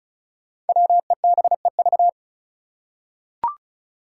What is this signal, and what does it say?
Morse code, heard as a keyed tone, about 36 wpm, 700 Hz. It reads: WEBEV
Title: Five Challenging Letter-Number Combinations 35wpm